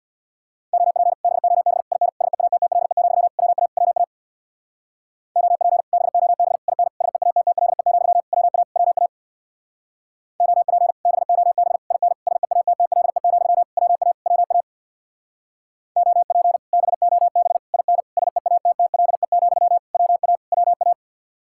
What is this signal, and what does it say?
Morse code, heard as a keyed tone, about 61 wpm, 700 Hz. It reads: OP BOB IN SEATTLE, WA WA OP BOB IN SEATTLE, WA WA OP BOB IN SEATTLE, WA WA OP BOB IN SEATTLE, WA WA
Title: Speed-Racing Common QSO Phrases Encore - 50wpm